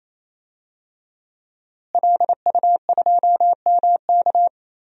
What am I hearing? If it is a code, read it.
Morse code, heard as a keyed tone, about 28 wpm, 700 Hz. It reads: LU2MK
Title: DXCC-100 Call Signs - 5 Characters - Repeated 3x - 28wpm